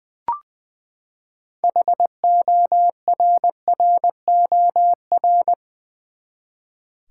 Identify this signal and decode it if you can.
Morse code, heard as a keyed tone, about 20 wpm, 700 Hz. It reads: HORROR
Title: Six Letter Words - 20wpm